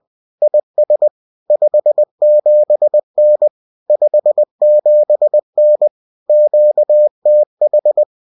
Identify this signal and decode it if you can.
Morse code, heard as a keyed tone, about 20 wpm, 600 Hz. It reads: IS 57N 57N QTH